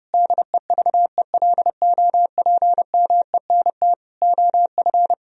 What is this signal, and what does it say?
Morse code, heard as a keyed tone, about 30 wpm, 700 Hz. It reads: DEVELOPMENT OF
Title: Sentences from Top 1000 Words - Review Part 4 - 30wpm